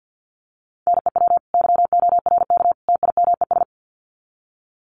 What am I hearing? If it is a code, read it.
Morse code, heard as a keyed tone, about 50 wpm, 700 Hz. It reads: NEW YORK TIMES
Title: Sets of 3 Words 50wpm